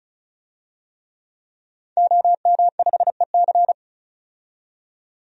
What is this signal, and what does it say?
Morse code, heard as a keyed tone, about 34 wpm, 700 Hz. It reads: OM5EC
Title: DXCC-100 Call Signs - 5 Characters - 35wpm